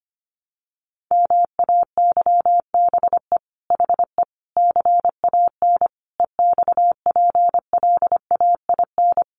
Morse code, heard as a keyed tone, about 25 wpm, 700 Hz. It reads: MAYBE HE CAN EXPLAIN
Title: Sentences from Top 1100 Words 25wpm